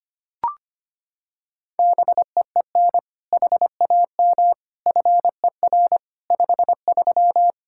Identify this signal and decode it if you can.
Morse code, heard as a keyed tone, about 25 wpm, 700 Hz. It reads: BEEN HAM FER 53
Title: Practice QSO - Exchange 3 - More Info - 25wpm